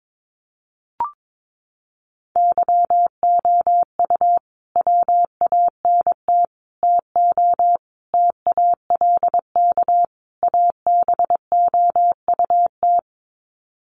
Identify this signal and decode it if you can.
Morse code, heard as a keyed tone, about 22 wpm, 700 Hz. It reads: YOU WANT TO TALK ABOUT
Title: Sets of 5 Words Encore 22wpm